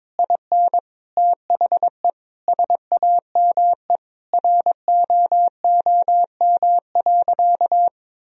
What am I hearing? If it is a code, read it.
Morse code, heard as a keyed tone, about 22 wpm, 700 Hz. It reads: IN THE SAME ROOM.
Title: Sentences from Top 1000 Words - Review Part 3 - 22wpm